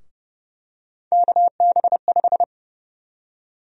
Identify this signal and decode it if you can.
Morse code, heard as a keyed tone, about 30 wpm, 700 Hz. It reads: KB5
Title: USA Call Sign Prefixes - Repeated 2x - 30wpm